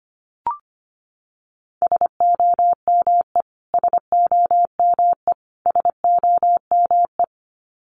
Morse code, heard as a keyed tone, about 25 wpm, 700 Hz. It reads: SOME SOME SOME
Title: Top 200 Words Repeated 25wpm